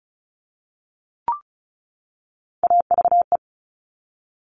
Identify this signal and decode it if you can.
Morse code, heard as a keyed tone, about 36 wpm, 700 Hz. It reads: AVE
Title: Top Three Letter Combinations in Words 35wpm